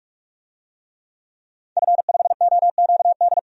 Morse code, heard as a keyed tone, about 46 wpm, 700 Hz. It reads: F5OQD
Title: DXCC-100 Call Signs - 5 Characters - 45wpm